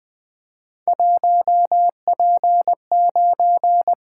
Morse code, heard as a keyed tone, about 20 wpm, 700 Hz. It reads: 1P9
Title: Three Letter-Number Combinations - 20wpm